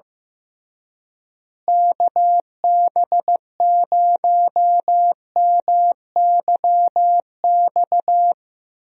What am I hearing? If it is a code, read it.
Morse code, heard as a keyed tone, about 15 wpm, 700 Hz. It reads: KB0MYX
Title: USA Call Signs - 6 Characters - Repeated 3x - 15wpm